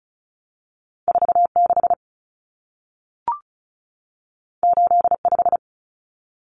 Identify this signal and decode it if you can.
Morse code, heard as a keyed tone, about 36 wpm, 700 Hz. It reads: 46 85
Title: Common Cut Numbers - 2-Digit 35wpm